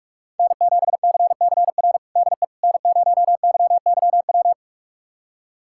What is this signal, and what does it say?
Morse code, heard as a keyed tone, about 46 wpm, 700 Hz. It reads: N7CCR DE N0YYW
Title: USA Callsign DE Callsign - 5 Characters - 45wpm